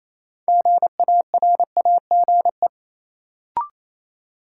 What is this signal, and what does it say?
Morse code, heard as a keyed tone, about 28 wpm, 700 Hz. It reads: GARAGE
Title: Sentences from Top 1600 Words - New Word Drill Alt - 28wpm